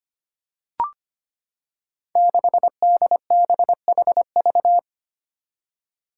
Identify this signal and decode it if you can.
Morse code, heard as a keyed tone, about 25 wpm, 700 Hz. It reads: 6DBHV
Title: Five Challenging Letter-Number Combinations 25wpm